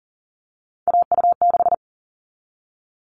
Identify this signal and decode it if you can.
Morse code, heard as a keyed tone, about 40 wpm, 700 Hz. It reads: AU6
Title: All Cut Numbers - 3-Digit 40wpm